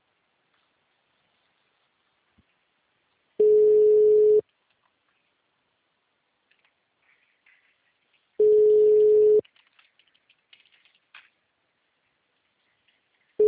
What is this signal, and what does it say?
Ringback tone (Europe, 425 Hz).